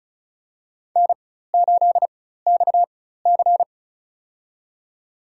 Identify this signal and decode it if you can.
Morse code, heard as a keyed tone, about 36 wpm, spaced as for 16 wpm, 700 Hz. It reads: N8XC
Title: USA Call Signs - 4 Characters - 4x Character Spacing - 35wpm